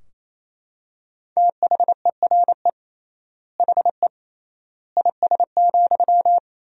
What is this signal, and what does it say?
Morse code, heard as a keyed tone, about 28 wpm, 700 Hz. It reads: THERE HE IS,
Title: 3x Word-Spacing - Sentences from Top 100 Words - 28wpm